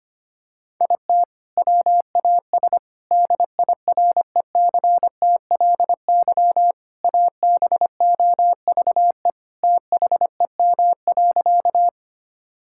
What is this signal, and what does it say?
Morse code, heard as a keyed tone, about 25 wpm, 700 Hz. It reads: IT WAS DIRECTLY ABOVE THEM.
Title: Sentences from Top 1000 Words - Review Part 5 - 25wpm